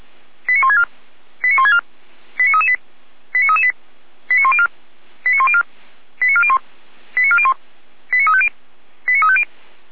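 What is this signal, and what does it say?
Five-tone selective call (ZVEI), heard as quick runs of five tones, tones of about 70 ms. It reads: 89164 89164 89338 89338 89104 89104 89401 89401 89350 89350
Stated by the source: ZVEI five-tone selective call received on SDR, code 89701.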